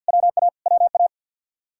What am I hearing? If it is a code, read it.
Morse code, heard as a keyed tone, about 60 wpm, 700 Hz. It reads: WA WA
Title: Speed-Racing Common QSO Phrases - 50wpm